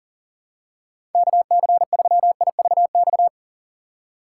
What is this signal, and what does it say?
Morse code, heard as a keyed tone, about 40 wpm, 700 Hz. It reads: KC3IVX
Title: USA Call Signs - 6 Characters - 4x Character Spacing - 40wpm